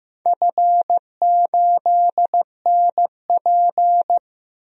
Morse code, heard as a keyed tone, about 16 wpm, 700 Hz. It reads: F8NP
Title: DXCC-100 Call Signs - 4 Characters - Repeated 3x - 15wpm